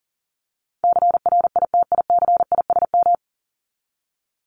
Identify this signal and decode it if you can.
Morse code, heard as a keyed tone, about 40 wpm, 700 Hz. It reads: CRITICISM
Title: Top 1001-1250 Words 40wpm